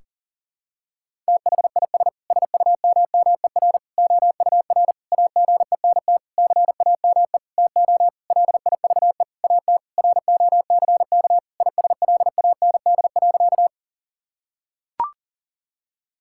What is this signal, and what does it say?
Morse code, heard as a keyed tone, about 40 wpm, 700 Hz. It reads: THIS SUMMER OUR AGENT CAME TO LIVE AT ROCK ISLAND.
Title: Sentences from Top 1600 Words 40wpm